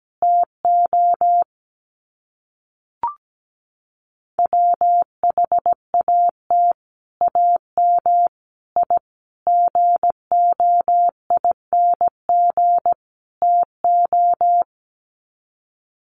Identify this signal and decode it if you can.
Morse code, heard as a keyed tone, about 17 wpm, 700 Hz. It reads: TO WHAT AM I GOING TO
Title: Sets of 5 Words Encore 17wpm